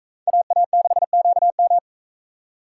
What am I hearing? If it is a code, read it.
Morse code, heard as a keyed tone, about 38 wpm, 700 Hz. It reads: AA6QM